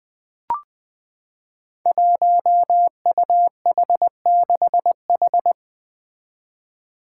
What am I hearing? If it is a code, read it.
Morse code, heard as a keyed tone, about 20 wpm, 700 Hz. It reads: 1UH6H
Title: Five Challenging Letter-Number Combinations 20wpm